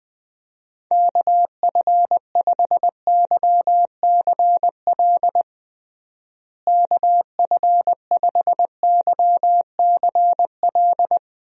Morse code, heard as a keyed tone, about 20 wpm, 700 Hz. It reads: KF5YCL KF5YCL
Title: USA Call Signs - 6 Characters - Repeated 3x - 20wpm